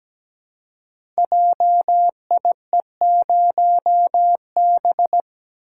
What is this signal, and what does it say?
Morse code, heard as a keyed tone, about 17 wpm, 700 Hz. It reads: JIE0B
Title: Code Groups - 5 Letter:Numbers - 17wpm